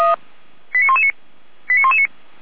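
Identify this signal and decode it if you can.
Five-tone selective call (ZVEI), heard as quick runs of five tones, tones of about 70 ms. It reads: 89119 89119
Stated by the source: ZVEI five-tone selective call received via SDR.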